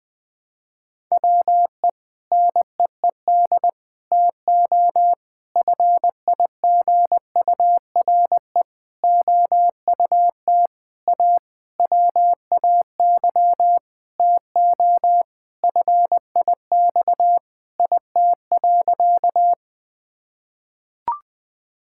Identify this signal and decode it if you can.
Morse code, heard as a keyed tone, about 20 wpm, 700 Hz. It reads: WE NEED TO FIGURE OUT A WAY TO FIX IT.
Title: Sentences with 6-Word Phrases - Spoken Beginning - 20wpm